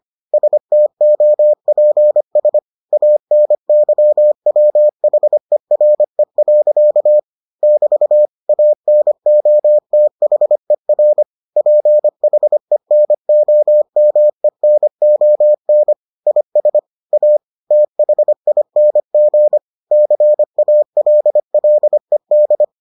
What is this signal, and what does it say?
Morse code, heard as a keyed tone, about 25 wpm, 600 Hz. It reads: STOPS ANYWHERE. <BT> ANOTHER PHENOMENON IS A THING CALLED